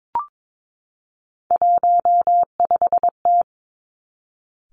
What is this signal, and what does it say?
Morse code, heard as a keyed tone, about 22 wpm, 700 Hz. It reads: 15T
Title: Three Challenging Letter-Number Combinations 22wpm